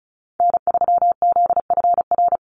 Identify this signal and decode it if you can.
Morse code, heard as a keyed tone, about 36 wpm, 700 Hz. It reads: N3ZFR
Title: USA Call Signs 35wpm